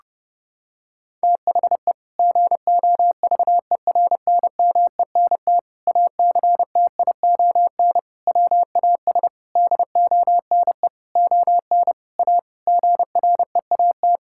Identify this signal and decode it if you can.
Morse code, heard as a keyed tone, about 30 wpm, 700 Hz. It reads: THE GOVERNMENT ACTION WAS DONE ON A GREAT
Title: Sentences from Top 1000 Words - Review Part 5 - 30wpm